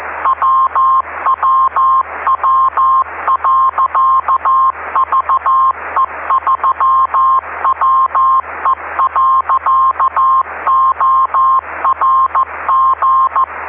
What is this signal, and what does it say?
Morse code, heard as a keyed tone, about 14 wpm, 1060 Hz, with noise. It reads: WWW.VE3WE.ORG
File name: ve3we-cw-beacon-28265